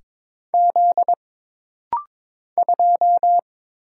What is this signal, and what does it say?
Morse code, heard as a keyed tone, about 22 wpm, 700 Hz. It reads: Z 2
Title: Single Letter-Number Alt - Rapid-Fire 22wpm